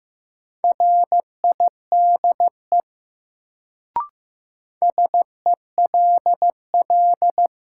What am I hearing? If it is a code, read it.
Morse code, heard as a keyed tone, about 15 wpm, 700 Hz. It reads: RIDE SELL
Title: Sentences from Top 1000 Words - Review Part 6 - New Word Drill - 15wpm